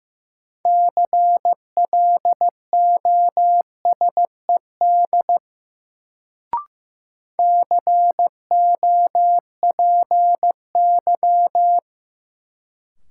Morse code, heard as a keyed tone, about 16 wpm, 700 Hz. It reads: CLOSED COPY
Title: Sentences from Top 1400 Words - New Word Drill Alt - 15wpm